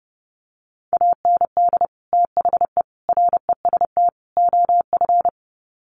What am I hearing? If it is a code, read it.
Morse code, heard as a keyed tone, about 28 wpm, 700 Hz. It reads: AND THE REST OF
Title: Sets of 4 Words 30wpm